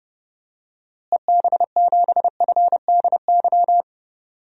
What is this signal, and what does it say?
Morse code, heard as a keyed tone, about 30 wpm, 700 Hz. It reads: EB7FDY
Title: DXCC-100 Call Signs - 6 Characters - Repeated 3x - 30wpm